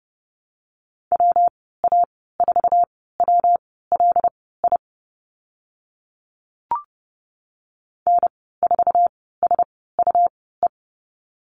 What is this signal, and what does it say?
Morse code, heard as a keyed tone, about 30 wpm, spaced as for 17 wpm, 700 Hz. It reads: WA4WLI N4SUE
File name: Lesson 19g - Letter w - Call Signs 30wpm